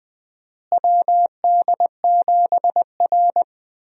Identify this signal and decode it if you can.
Morse code, heard as a keyed tone, about 21 wpm, 700 Hz. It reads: WD7R
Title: USA Call Signs - 4 Characters - Repeated 3x - 20wpm